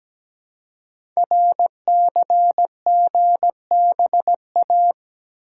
Morse code, heard as a keyed tone, about 17 wpm, 700 Hz. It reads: RCGBA
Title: Code Groups - 5 Letters - 17wpm